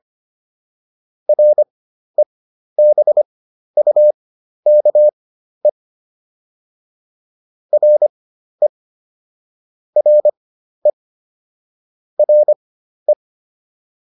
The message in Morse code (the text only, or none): REBUKE RE RE RE